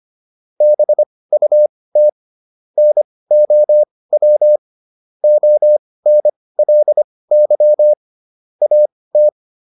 Morse code, heard as a keyed tone, about 25 wpm, spaced as for 18 wpm, 600 Hz. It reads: BUT NOW ONLY AT